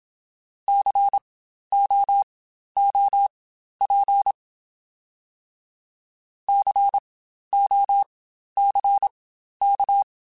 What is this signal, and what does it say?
Morse code, heard as a keyed tone, about 27 wpm, spaced as for 12 wpm, 800 Hz. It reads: COOP COCK